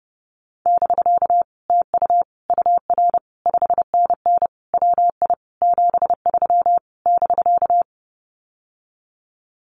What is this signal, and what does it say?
Morse code, heard as a keyed tone, about 30 wpm, 700 Hz. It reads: <BK> TU UR 5NN WI 73 <BK>
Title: Common QSO Phrases Encore 30wpm